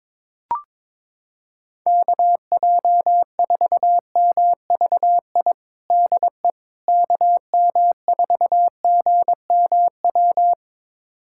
Morse code, heard as a keyed tone, about 22 wpm, 700 Hz. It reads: KJ4MVI DE KM4GMW
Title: USA Callsign DE Callsign - 6 Characters - 22wpm